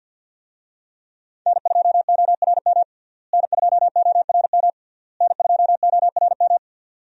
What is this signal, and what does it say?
Morse code, heard as a keyed tone, about 52 wpm, 700 Hz. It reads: N2ORM N2ORM N2ORM
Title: USA Call Signs - 5 Characters - Repeated 3x - 50wpm